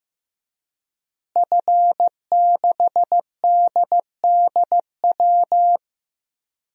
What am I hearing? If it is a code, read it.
Morse code, heard as a keyed tone, about 16 wpm, 700 Hz. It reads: F6DDW
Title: DXCC-100 Call Signs - 5 Characters - Repeated 2x - 15wpm